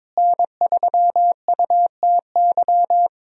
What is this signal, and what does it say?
Morse code, heard as a keyed tone, about 22 wpm, 700 Hz. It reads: N3UTY
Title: USA Call Signs - 5 Characters - Repeated 2x - 22wpm